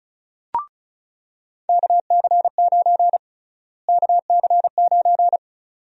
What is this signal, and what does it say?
Morse code, heard as a keyed tone, about 34 wpm, 700 Hz. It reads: KC9 KC9
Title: USA Call Sign Prefixes - Repeated 2x - 35wpm